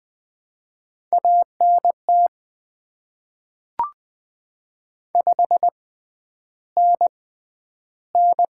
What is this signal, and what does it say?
Morse code, heard as a keyed tone, about 20 wpm, spaced as for 10 wpm, 700 Hz. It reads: ANT 5NN
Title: Top 20 QSO Elements - 6x Character Spacing - 20wpm